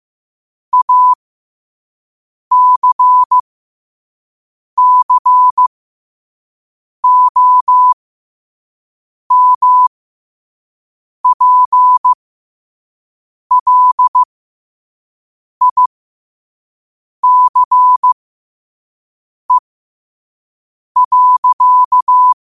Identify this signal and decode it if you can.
Morse code, heard as a keyed tone, about 15 wpm, spaced as for 5 wpm, 1000 Hz. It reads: ACCOMPLICE.
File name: qotdfiles / 5wpm